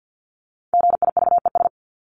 Morse code, heard as a keyed tone, about 50 wpm, 700 Hz. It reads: GIVES
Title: ES:S Suffix - Top 100 Words with Base Words 50wpm